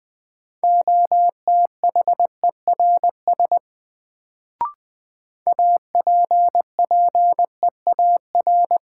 Morse code, heard as a keyed tone, about 20 wpm, 700 Hz. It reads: OTHERS APPEAR
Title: Sentences from Top 1000 Words - Review Part 2 - New Word Drill - 20wpm